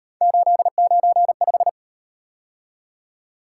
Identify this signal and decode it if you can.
Morse code, heard as a keyed tone, about 38 wpm, 700 Hz. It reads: 895